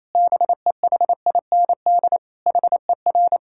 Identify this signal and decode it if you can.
Morse code, heard as a keyed tone, about 27 wpm, 700 Hz. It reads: BEHIND HER
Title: Sentences from Top 1000 Words - Review Part 6 - 28wpm